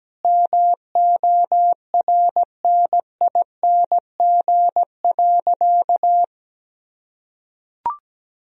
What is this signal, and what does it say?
Morse code, heard as a keyed tone, about 17 wpm, 700 Hz. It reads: MORNING.
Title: Sentences from Top 1000 Words - Review Part 1 - 17wpm